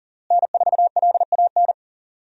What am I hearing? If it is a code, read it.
Morse code, heard as a keyed tone, about 40 wpm, 700 Hz. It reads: N4LAN
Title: USA Call Signs - 5 Characters - Repeated 2x - 40wpm